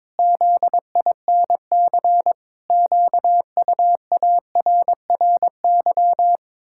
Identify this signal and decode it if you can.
Morse code, heard as a keyed tone, about 22 wpm, 700 Hz. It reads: ZINC QUARRY
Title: Pangram Challenge v2 - 22wpm